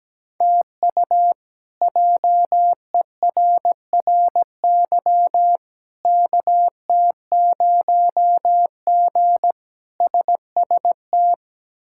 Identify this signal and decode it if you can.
Morse code, heard as a keyed tone, about 17 wpm, 700 Hz. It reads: TU JERRY KT0G SST
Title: K1USN SST QSO Practice - 17wpm